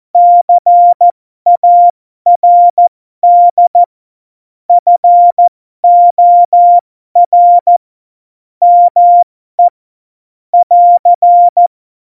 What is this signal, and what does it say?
Morse code, heard as a keyed tone, about 14 wpm, spaced as for 12 wpm, 700 Hz. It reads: CARD FOR ME <AR>
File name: Morsekurs / Witze